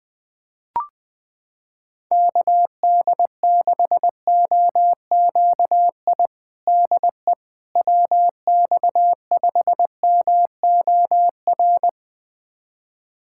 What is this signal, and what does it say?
Morse code, heard as a keyed tone, about 20 wpm, 700 Hz. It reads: KD6OQI DE WX5MOR
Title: USA Callsign DE Callsign - 6 Characters - 20wpm